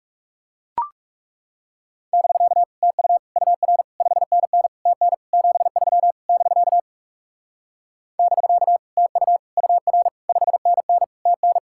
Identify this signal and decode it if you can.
Morse code, heard as a keyed tone, about 39 wpm, 700 Hz. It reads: <BK> TU UR 5NN TN 73 <BK> <BK> TU UR 5NN TN